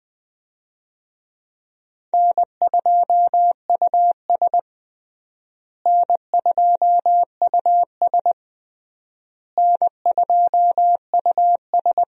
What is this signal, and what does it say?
Morse code, heard as a keyed tone, about 20 wpm, 700 Hz. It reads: N2US N2US N2US
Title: USA Call Signs - 4 Characters - Repeated 3x - 20wpm